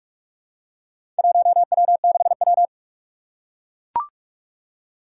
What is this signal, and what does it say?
Morse code, heard as a keyed tone, about 44 wpm, 700 Hz. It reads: 1W6W
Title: Four Challenging Letter-Number Combinations 45wpm